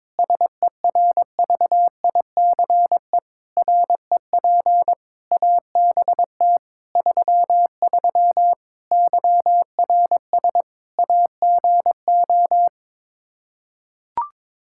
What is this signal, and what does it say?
Morse code, heard as a keyed tone, about 22 wpm, 700 Hz. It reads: SERVICE REP ABT 33 YRS AGO
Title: Practice QSO - Exchange 3 - More Info - 22wpm